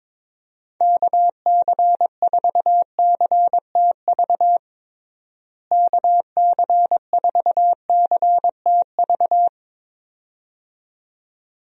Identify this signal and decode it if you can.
Morse code, heard as a keyed tone, about 21 wpm, 700 Hz. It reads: KC4CTV KC4CTV
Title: USA Call Signs - 6 Characters - Repeated 2x - 22wpm